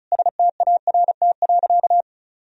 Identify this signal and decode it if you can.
Morse code, heard as a keyed tone, about 36 wpm, 700 Hz. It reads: START.
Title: Sentences from Top 1000 Words - Review Part 1 - 35wpm